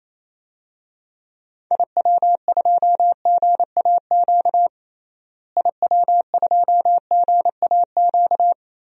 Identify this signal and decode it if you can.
Morse code, heard as a keyed tone, about 28 wpm, 700 Hz. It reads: IW2GAQ IW2GAQ
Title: DXCC-100 Call Signs - 6 Characters - Repeated 3x - 28wpm